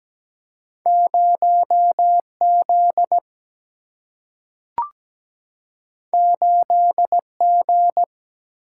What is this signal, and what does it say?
Morse code, heard as a keyed tone, about 17 wpm, 700 Hz. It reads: 0 Z 8 G
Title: Single-Letter Single-Number 17wpm